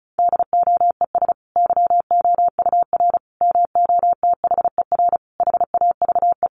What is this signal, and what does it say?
Morse code, heard as a keyed tone, about 34 wpm, 700 Hz. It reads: DOES YOUR MOTHER HAVE
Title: Sentences from Top 1400 Words 35wpm